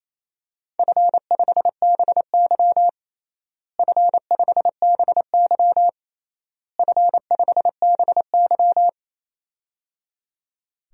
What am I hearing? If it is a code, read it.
Morse code, heard as a keyed tone, about 28 wpm, 700 Hz. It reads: F5BY F5BY F5BY
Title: DXCC-100 Call Signs - 4 Characters - Repeated 3x - 28wpm